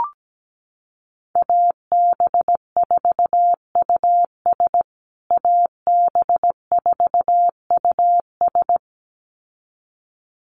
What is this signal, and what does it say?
Morse code, heard as a keyed tone, about 17 wpm, 700 Hz. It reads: AB4US AB4US
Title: USA Call Signs Repeated 2x 17wpm